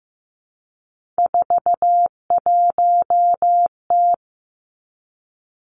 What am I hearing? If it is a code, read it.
Morse code, heard as a keyed tone, about 15 wpm, 700 Hz. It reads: 41T
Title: Three Challenging Letter-Number Combinations 15wpm